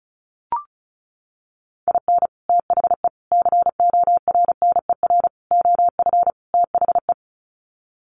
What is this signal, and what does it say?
Morse code, heard as a keyed tone, about 34 wpm, 700 Hz. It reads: IN THE CORNER OF THE
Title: Sets of 5 Words Encore 35wpm